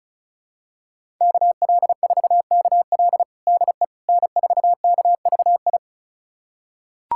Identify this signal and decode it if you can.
Morse code, heard as a keyed tone, about 36 wpm, 700 Hz. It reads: KL4KL DE N4KVI
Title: USA Callsign DE Callsign - 5 Characters - 35wpm